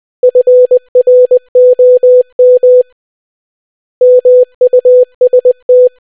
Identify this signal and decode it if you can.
Morse code, heard as a keyed tone, about 20 wpm, 500 Hz. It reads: FROM MUST